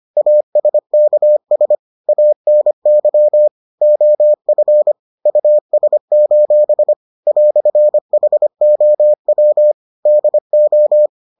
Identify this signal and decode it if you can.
Morse code, heard as a keyed tone, about 25 wpm, 600 Hz. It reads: ASKS ANY OF US: "HOW DO